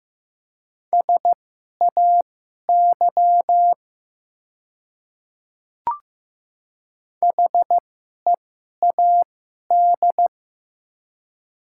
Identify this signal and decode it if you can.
Morse code, heard as a keyed tone, about 15 wpm, spaced as for 11 wpm, 700 Hz. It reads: SAY HEAD
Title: 2x Character Spacing - Top 300 Words - 15wpm